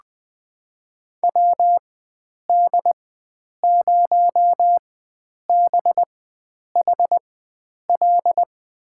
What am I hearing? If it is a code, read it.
Morse code, heard as a keyed tone, about 20 wpm, spaced as for 9 wpm, 700 Hz. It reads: WD0BHL
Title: USA Call Signs - 6 Characters - 4x Character Spacing - 20wpm